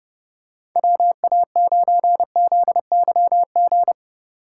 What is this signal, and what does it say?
Morse code, heard as a keyed tone, about 30 wpm, 700 Hz. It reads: WA9ZYG
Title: USA Call Signs - 8x Character Spacing - 30wpm